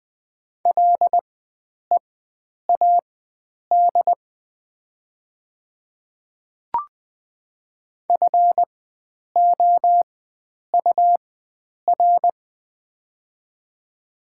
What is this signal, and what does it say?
Morse code, heard as a keyed tone, about 20 wpm, spaced as for 9 wpm, 700 Hz. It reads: LEAD FOUR
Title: 4x Character Spacing - Top 500 Words - 20wpm